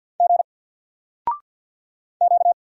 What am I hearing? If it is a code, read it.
Morse code, heard as a keyed tone, about 48 wpm, 700 Hz. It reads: G Q
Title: Single Character (Letter:Number:Punctuation:Prosign) 50wpm